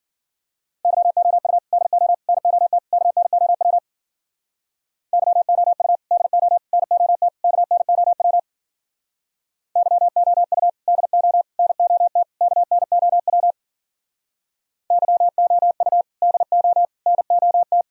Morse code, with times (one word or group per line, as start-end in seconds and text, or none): 0.8-1.6 YOU
1.7-2.1 DO
2.3-2.8 NOT
2.9-3.8 KNOW
5.1-6.0 YOU
6.1-6.6 DO
6.7-7.3 NOT
7.4-8.4 KNOW
9.8-10.7 YOU
10.9-11.4 DO
11.6-12.2 NOT
12.4-13.5 KNOW
14.9-16.0 YOU
16.2-16.8 DO
17.1-17.8 NOT